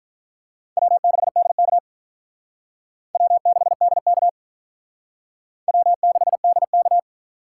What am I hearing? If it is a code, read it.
Morse code, heard as a keyed tone, about 46 wpm, 700 Hz. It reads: W6DK W6DK W6DK